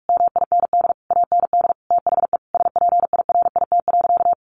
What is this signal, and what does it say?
Morse code, heard as a keyed tone, about 44 wpm, 700 Hz. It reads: MIND AND THE SPIRIT.
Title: Sentences from Top 1400 Words 45wpm